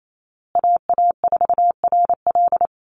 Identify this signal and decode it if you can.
Morse code, heard as a keyed tone, about 28 wpm, 700 Hz. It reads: AA4RL
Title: USA Call Signs Repeated 2x 28wpm